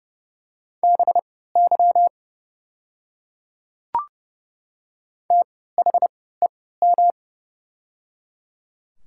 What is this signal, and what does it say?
Morse code, heard as a keyed tone, about 31 wpm, spaced as for 17 wpm, 700 Hz. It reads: BY THEM
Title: 3x Character Spacing - Top 100 Words - 30wpm